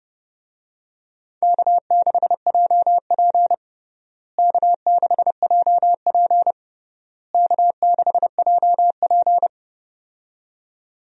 Morse code, heard as a keyed tone, about 30 wpm, 700 Hz. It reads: K6JP K6JP K6JP
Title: USA Call Signs - 4 Characters - Repeated 3x - 30wpm